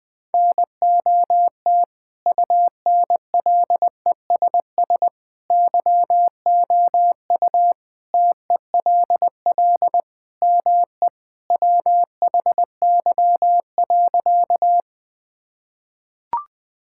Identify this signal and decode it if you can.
Morse code, heard as a keyed tone, about 20 wpm, 700 Hz. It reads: NOT UNLESS YOU TELL ME WHY.
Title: Sentences from Top 1000 Words - Review Part 6 - 20wpm